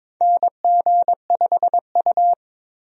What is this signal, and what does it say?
Morse code, heard as a keyed tone, about 21 wpm, 700 Hz. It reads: NG5U
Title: USA Call Signs - 4 Characters - 22wpm